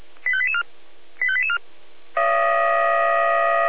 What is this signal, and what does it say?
Five-tone selective call (ZVEI), heard as quick runs of five tones, tones of about 70 ms. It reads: 85884 85884